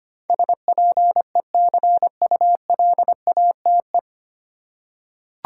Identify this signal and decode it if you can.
Morse code, heard as a keyed tone, about 25 wpm, 700 Hz. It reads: SPECULATE
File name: practice / 25wpm / 9 Letter Words - First 3 Letters Given - 25wpm